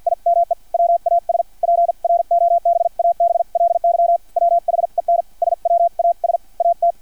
Morse code, heard as a keyed tone, about 50 wpm, spaced as for 43 wpm, 690 Hz. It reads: IME WAS PROBABLY WHEN SWAS AT
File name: archived / CRSnet